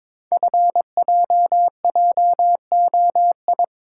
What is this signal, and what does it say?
Morse code, heard as a keyed tone, about 22 wpm, 700 Hz. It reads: FJJOI
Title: Code Groups - 5 Letters - 22wpm